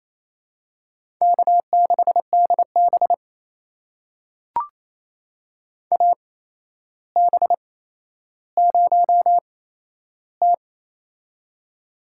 Morse code, heard as a keyed tone, about 28 wpm, spaced as for 12 wpm, 700 Hz. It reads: K6DB AB0T
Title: USA Call Signs - 4 Characters - 8x Character Spacing - 28wpm